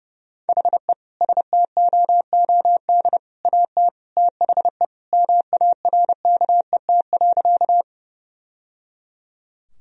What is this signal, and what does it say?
Morse code, heard as a keyed tone, about 30 wpm, 700 Hz. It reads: HE STOOD AT THE MARKET.
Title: Sentences from Top 1000 Words - Review Part 3 - 30wpm